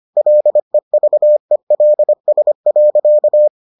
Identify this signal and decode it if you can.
Morse code, heard as a keyed tone, about 25 wpm, 600 Hz. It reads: LEVELS.